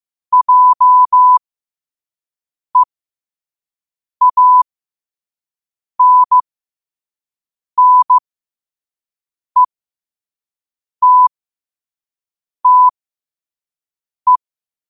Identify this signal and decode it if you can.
Morse code, heard as a keyed tone, about 15 wpm, spaced as for 5 wpm, 1000 Hz. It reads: JEANNETTE